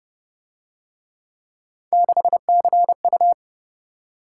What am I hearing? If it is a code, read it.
Morse code, heard as a keyed tone, about 28 wpm, 700 Hz. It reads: 6CU